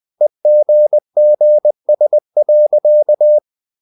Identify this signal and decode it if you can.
Morse code, heard as a keyed tone, about 20 wpm, 600 Hz. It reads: EGGS.
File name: CW / Practice / EnglishSayings20